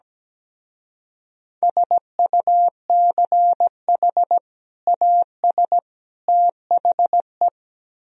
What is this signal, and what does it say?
Morse code, heard as a keyed tone, about 18 wpm, 700 Hz. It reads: SUCH AS THE
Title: Speed-Racing - Sets of 3 Words 17wpm